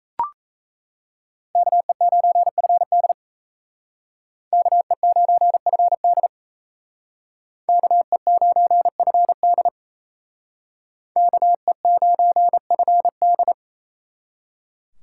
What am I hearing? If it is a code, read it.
Morse code, heard as a keyed tone, about 36 wpm, 700 Hz. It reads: KE9FD KE9FD KE9FD KE9FD